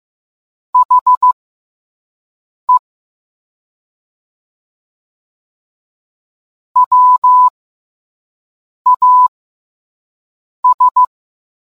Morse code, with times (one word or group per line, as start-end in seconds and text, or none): 0.7-2.8 HE
6.8-11.0 WAS